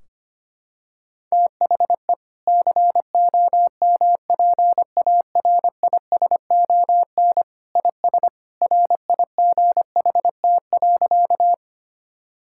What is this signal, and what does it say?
Morse code, heard as a keyed tone, about 25 wpm, 700 Hz. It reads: THE COMPARISON IS RIGHT.
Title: Sentences from Top 1600 Words 25wpm